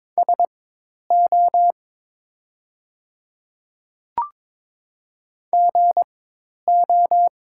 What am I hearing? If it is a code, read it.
Morse code, heard as a keyed tone, about 22 wpm, spaced as for 10 wpm, 700 Hz. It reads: SO GO
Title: 4x Character Spacing - Top 100 Words - 22wpm